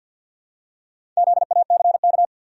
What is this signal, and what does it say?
Morse code, heard as a keyed tone, about 53 wpm, 700 Hz. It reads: ZACK
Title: USA Common Names - Max 4 Letters - 50wpm